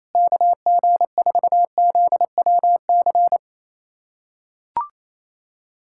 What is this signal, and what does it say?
Morse code, heard as a keyed tone, about 28 wpm, 700 Hz. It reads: KG4ZWC
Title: USA Callsign DE Callsign - 6 Characters - 28wpm